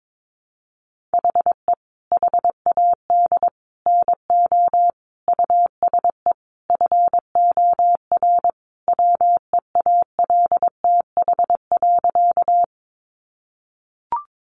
Morse code, heard as a keyed tone, about 22 wpm, 700 Hz. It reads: HE HAD NO USE FOR WEALTH.
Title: Sentences from Top 1200 Words 22wpm